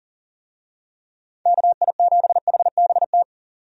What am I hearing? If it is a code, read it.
Morse code, heard as a keyed tone, about 38 wpm, 700 Hz. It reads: KI7HBT
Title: USA Call Signs - 6 Characters - 40wpm